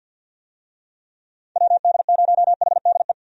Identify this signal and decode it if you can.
Morse code, heard as a keyed tone, about 48 wpm, 700 Hz. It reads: WD0HDE
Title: USA Call Signs - 6 Characters - 50wpm